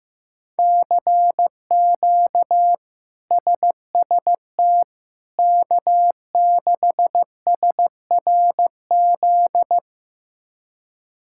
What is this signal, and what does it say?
Morse code, heard as a keyed tone, about 15 wpm, 700 Hz. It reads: CQ SST K6SRZ
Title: K1USN SST QSO Practice - 15wpm